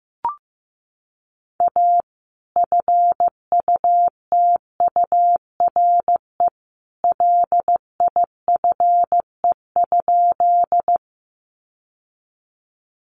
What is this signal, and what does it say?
Morse code, heard as a keyed tone, about 15 wpm, 700 Hz. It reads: A FUTURE LIFE?
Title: Sentences from Top 700 Words 15wpm